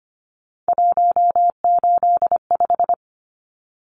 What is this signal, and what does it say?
Morse code, heard as a keyed tone, about 25 wpm, 700 Hz. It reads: 185
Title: Numbers - 3-Digit 25wpm